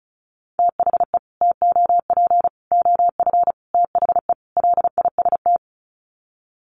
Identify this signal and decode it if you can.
Morse code, heard as a keyed tone, about 34 wpm, 700 Hz. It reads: THE TOP OF THE LIST
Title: Sets of 5 Words Encore2 35wpm